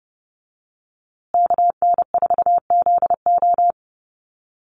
Morse code, heard as a keyed tone, about 30 wpm, 700 Hz. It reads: KN4ZO
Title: USA Call Signs Repeated 2x 30wpm